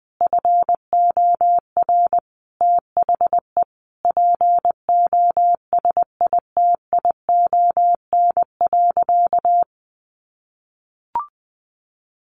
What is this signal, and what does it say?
Morse code, heard as a keyed tone, about 20 wpm, 700 Hz. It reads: FOR THE POSITION.
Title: Sentences from Top 1400 Words 20wpm